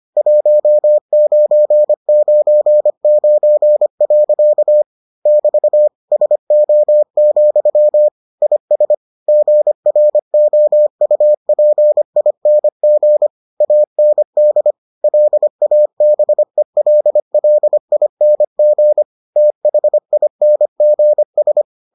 Morse code, heard as a keyed tone, about 25 wpm, 600 Hz. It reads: 1999. <BT> SO, IS GROUPING AND LABELLING THINGS